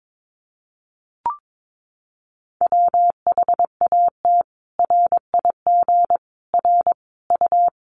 Morse code, heard as a keyed tone, about 22 wpm, 700 Hz. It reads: WHAT RIG R U
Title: Common QSO Phrases 22wpm